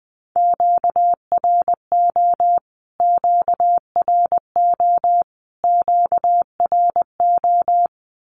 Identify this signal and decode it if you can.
Morse code, heard as a keyed tone, about 20 wpm, 700 Hz. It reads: QRO QRO QRO
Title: Q-Codes Repeated 3x 20wpm